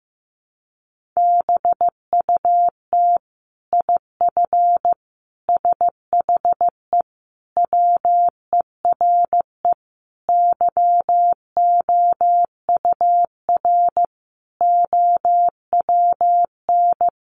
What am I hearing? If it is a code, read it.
Morse code, heard as a keyed tone, about 15 wpm, 700 Hz. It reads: BUT IF SHE WERE YOUR OWN
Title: Sentences from Top 1500 Words 15wpm